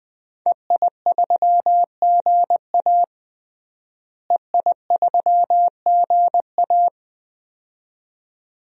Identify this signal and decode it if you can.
Morse code, heard as a keyed tone, about 21 wpm, 700 Hz. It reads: EI3GA EI3GA
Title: DXCC-100 Call Signs - 5 Characters - Repeated 2x - 20wpm